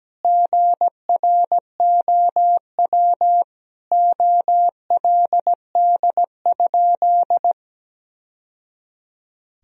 Morse code, heard as a keyed tone, about 17 wpm, 700 Hz. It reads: GROW OLD?
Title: Sentences from Top 1000 Words - Review Part 1 - 17wpm